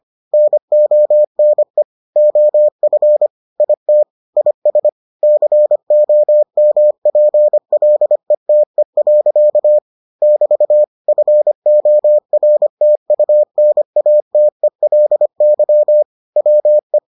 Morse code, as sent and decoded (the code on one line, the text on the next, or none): -. --- -. / --- ..-. / .. - / .. ... / -.-. --- -- .--. .-.. - . .-.-.- / -...- / ..-. --- .-. - ..- -. .- - . .-.. -.-- / .-- .
NONE OF IT IS COMPLETE. <BT> FORTUNATELY WE